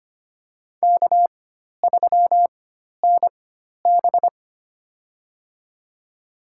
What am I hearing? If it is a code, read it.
Morse code, heard as a keyed tone, about 25 wpm, spaced as for 12 wpm, 700 Hz. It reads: K3NB
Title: USA Call Signs - 4 Characters - 4x Character Spacing - 25wpm